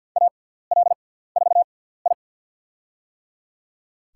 Morse code, heard as a keyed tone, about 48 wpm, spaced as for 17 wpm, 690 Hz. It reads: AL4I
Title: USA Call Signs - 4 Characters - 6x Character Spacing - 50wpm